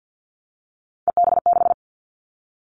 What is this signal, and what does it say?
Morse code, heard as a keyed tone, about 50 wpm, 700 Hz. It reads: EB6
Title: Callsign Prefixes for Top 100 DXCC - 50wpm